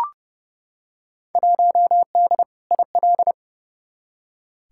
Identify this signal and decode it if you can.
Morse code, heard as a keyed tone, about 30 wpm, 700 Hz. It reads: 1D IL
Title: Field Day Exchanges - 30wpm